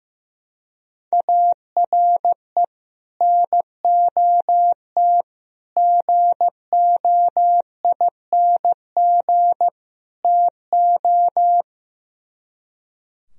Morse code, heard as a keyed tone, about 15 wpm, 700 Hz. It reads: ARE NOT GOING TO
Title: Speed-Racing - Sets of 4 Words 15wpm